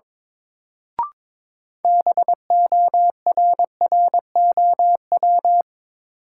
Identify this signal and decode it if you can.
Morse code, heard as a keyed tone, about 22 wpm, 700 Hz. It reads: BORROW
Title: Sentences from Top 1500 Words - New Word Drill Alt - 22wpm